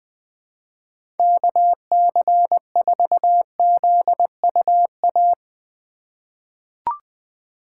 Morse code, heard as a keyed tone, about 20 wpm, 700 Hz. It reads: KC4ZUA
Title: 2023 POTA Hunter Call Signs - 20wpm